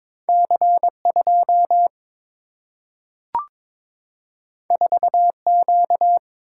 Morse code, heard as a keyed tone, about 22 wpm, 700 Hz. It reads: C 2 4 Q
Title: Single-Letter Single-Number 22wpm